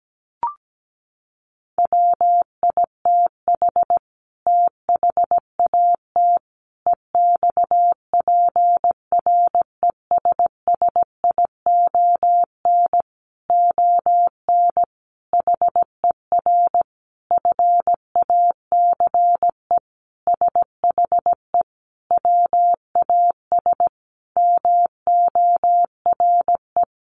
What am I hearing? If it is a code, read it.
Morse code, heard as a keyed tone, about 17 wpm, 700 Hz. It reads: WITH THAT EXPRESSION ON HER FACE SHE WAS MORE
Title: Sentences from Top 1200 Words 17wpm